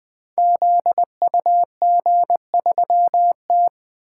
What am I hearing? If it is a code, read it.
Morse code, heard as a keyed tone, about 20 wpm, 700 Hz. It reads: ZUG3T
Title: Code Groups - 5 Letter:Numbers - 20wpm